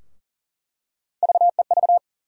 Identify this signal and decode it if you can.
Morse code, heard as a keyed tone, about 40 wpm, 700 Hz. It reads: VEV
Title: USA Call Sign Postfixes - 3 Letters 40wpm